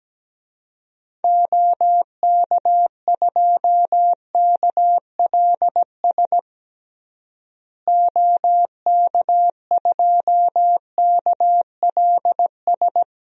Morse code, heard as a keyed tone, about 17 wpm, 700 Hz. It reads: OK2KLS OK2KLS
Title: DXCC-100 Call Signs - 6 Characters - Repeated 3x - 17wpm